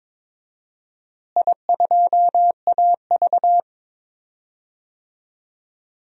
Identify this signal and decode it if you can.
Morse code, heard as a keyed tone, about 21 wpm, 700 Hz. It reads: I2AV
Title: DXCC-100 Call Signs - 4 Characters - Repeated 2x - 22wpm